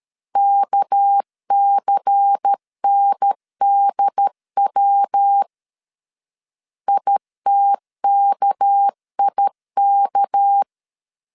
Morse code, heard as a keyed tone, about 13 wpm, 800 Hz. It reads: KCNDW ITKIK
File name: LEZIONI / 6 PKX0 / 6 PK 12wpm 2006